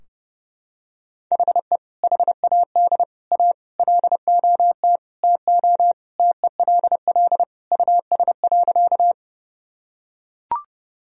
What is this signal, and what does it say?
Morse code, heard as a keyed tone, about 30 wpm, 700 Hz. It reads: HE HAD A LOT TO TELL US.
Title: Sentences from Top 1000 Words - Review Part 3 - 30wpm